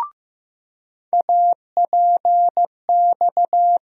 Morse code, heard as a keyed tone, about 15 wpm, 700 Hz. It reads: APX
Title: USA Call Sign Postfixes - 3 Letters 15wpm